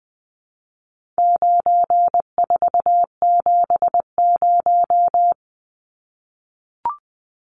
Morse code, heard as a keyed tone, about 20 wpm, 700 Hz. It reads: 9470
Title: Numbers - 4-Digit 20wpm